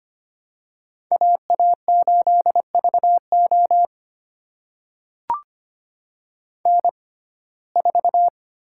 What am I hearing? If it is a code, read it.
Morse code, heard as a keyed tone, about 25 wpm, spaced as for 18 wpm, 700 Hz. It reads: AA8VO N4
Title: USA Call Signs - 5 Characters - 6x Character Spacing - 25wpm